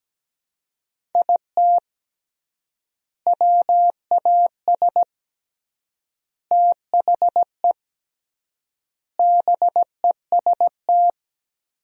Morse code, heard as a keyed tone, about 17 wpm, 700 Hz. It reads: IT WAS THE BEST
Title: 3x Word-Spacing - Sentences from Top 400 Words - 17wpm